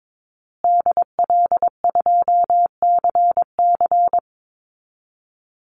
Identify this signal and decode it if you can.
Morse code, heard as a keyed tone, about 22 wpm, 700 Hz. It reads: DL2CC
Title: Call Signs Repeated 2x - 22wpm